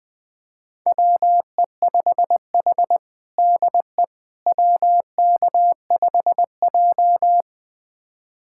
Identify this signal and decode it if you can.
Morse code, heard as a keyed tone, about 20 wpm, 700 Hz. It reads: WE5H DE WK5J
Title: USA Callsign DE Callsign - 4 Characters - 20wpm